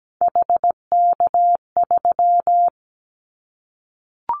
Morse code, heard as a keyed tone, about 17 wpm, 700 Hz. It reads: HK3
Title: Callsign Prefixes for Top 100 DXCC - 17wpm